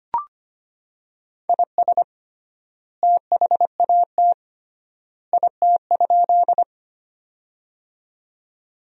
Morse code, heard as a keyed tone, about 25 wpm, 700 Hz. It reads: IS THAT IT?
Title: 3x Word-Spacing - Sentences from Top 100 Words - 25wpm